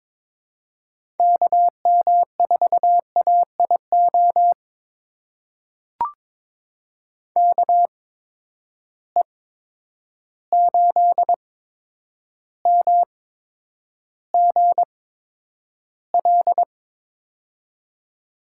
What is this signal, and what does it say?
Morse code, heard as a keyed tone, about 22 wpm, spaced as for 9 wpm, 700 Hz. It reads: KM4AIO KE8MGL
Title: USA Call Signs - 6 Characters - 8x Character Spacing - 22wpm